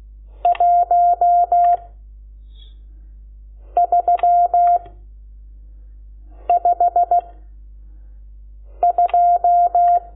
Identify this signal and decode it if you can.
Morse code, heard as a keyed tone, about 16 wpm, 700 Hz. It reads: 1 3 5 2
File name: Lecciones / CLASE Nº12